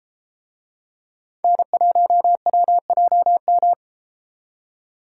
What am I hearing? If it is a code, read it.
Morse code, heard as a keyed tone, about 32 wpm, 700 Hz. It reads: N1WJM